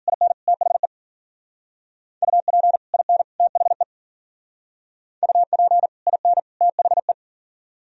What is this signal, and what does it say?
Morse code, heard as a keyed tone, about 48 wpm, 700 Hz. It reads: IN THE UP IN THE UP IN THE
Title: Speed-Racing - Sets of 3 Words 40wpm